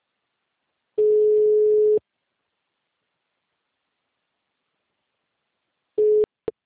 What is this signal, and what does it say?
Ringback tone (Europe, 425 Hz).